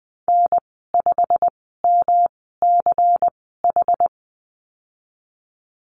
Morse code, heard as a keyed tone, about 20 wpm, spaced as for 15 wpm, 700 Hz. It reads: N5MCH
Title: 179 Lesson 29g - Slash - Call Signs 20wpm